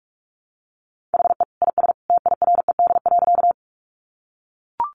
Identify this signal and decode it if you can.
Morse code, heard as a keyed tone, about 44 wpm, 700 Hz. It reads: HE IS TIRED.
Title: Additional Easy Phrases 45wpm